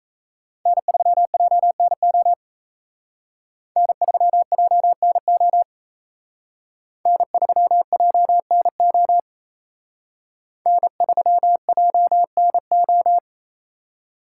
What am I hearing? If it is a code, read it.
Morse code, heard as a keyed tone, about 35 wpm, 700 Hz. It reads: N3JNO N3JNO N3JNO N3JNO